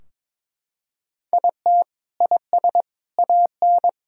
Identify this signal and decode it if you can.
Morse code, heard as a keyed tone, about 22 wpm, 700 Hz. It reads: IT IS AN
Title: Sentences from Top 1000 Words - Review Part 4 - 22wpm